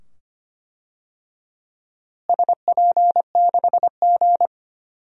Morse code, heard as a keyed tone, about 25 wpm, 700 Hz. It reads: SP6G
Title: DXCC-100 Call Signs - 4 Characters - Repeated 2x - 25wpm